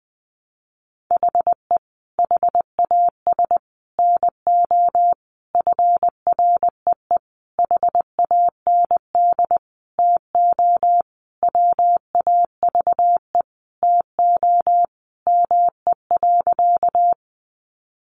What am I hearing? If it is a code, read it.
Morse code, heard as a keyed tone, about 20 wpm, 700 Hz. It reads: HE HAS NO FREE HAND TO WAVE TO ME.
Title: Sentences from Top 1400 Words 20wpm